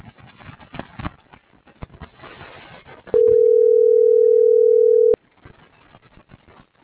Ringback tone (North America), 440 + 480 Hz.
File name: CENTRAL TERRESTRE